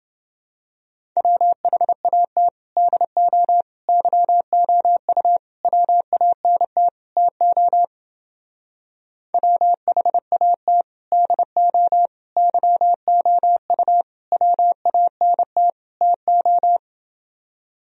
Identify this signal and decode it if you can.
Morse code, heard as a keyed tone, about 28 wpm, 700 Hz. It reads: WHAT DO YOU WANT TO WHAT DO YOU WANT TO